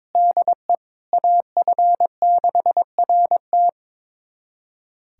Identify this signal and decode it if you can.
Morse code, heard as a keyed tone, about 22 wpm, 700 Hz. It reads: DE AF6RT
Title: USA Callsign DE Callsign - 5 Characters - 22wpm